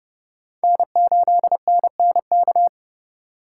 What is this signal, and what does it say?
Morse code, heard as a keyed tone, about 30 wpm, 700 Hz. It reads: N8NNK
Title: USA Call Signs - 3x Character Spacing - 30wpm